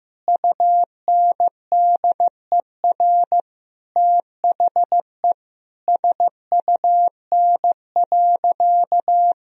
Morse code, heard as a keyed tone, about 15 wpm, 700 Hz. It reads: UNDER THE SUN.
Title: Sentences from Top 1000 Words - Review Part 1 - 15wpm